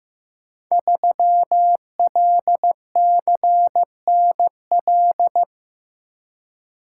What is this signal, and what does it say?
Morse code, heard as a keyed tone, about 15 wpm, 700 Hz. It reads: 3LCNL
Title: Code Groups - 5 Letter:Numbers - 15wpm